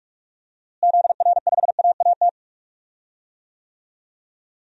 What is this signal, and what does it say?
Morse code, heard as a keyed tone, about 45 wpm, 700 Hz. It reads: ZR5AAT
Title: DXCC-100 Call Signs - 6 Characters - Repeated 2x - 45wpm